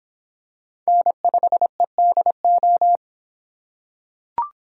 Morse code, heard as a keyed tone, about 25 wpm, 700 Hz. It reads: N5EDO